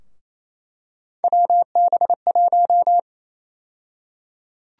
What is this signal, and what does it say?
Morse code, heard as a keyed tone, about 28 wpm, 700 Hz. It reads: WB1
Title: USA Call Sign Prefixes - 28wpm